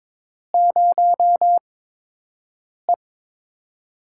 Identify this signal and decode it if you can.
Morse code, heard as a keyed tone, about 22 wpm, spaced as for 6 wpm, 700 Hz. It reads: 0E
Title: USA Call Signs - 4 Characters - 8x Character Spacing - 22wpm